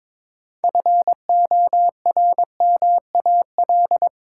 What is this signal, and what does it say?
Morse code, heard as a keyed tone, about 22 wpm, 700 Hz. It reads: FORMAL
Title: Six Letter Words - 22wpm